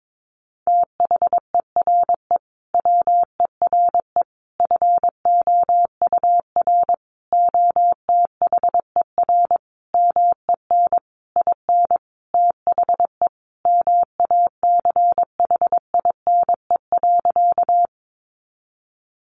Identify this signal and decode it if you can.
Morse code, heard as a keyed tone, about 22 wpm, 700 Hz. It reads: THERE WERE FOUR OTHER MEN IN THE MACHINE.
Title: A Sound Of Thunder 22wpm